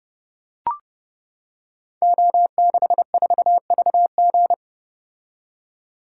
Morse code, heard as a keyed tone, about 30 wpm, 700 Hz. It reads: O64VG
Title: Code Groups - 5 Letter:Numbers - 30wpm